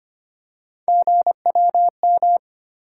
Morse code, heard as a keyed tone, about 25 wpm, 700 Hz. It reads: GWM
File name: Three Letter-Number Combinations - 25wpm